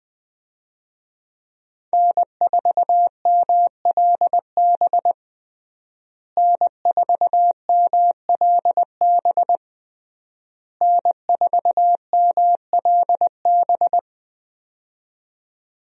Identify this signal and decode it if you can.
Morse code, heard as a keyed tone, about 20 wpm, 700 Hz. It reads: N4MLB N4MLB N4MLB
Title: USA Call Signs - 5 Characters - Repeated 3x - 20wpm